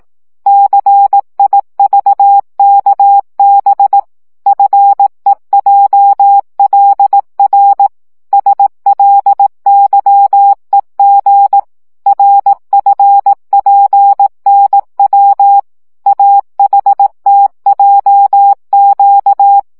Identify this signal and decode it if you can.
Morse code, heard as a keyed tone, about 18 wpm, 800 Hz. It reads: CIVKB FEJLR SLYEG RFPNW AHTJQ
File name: morze vizsga 2011-3 15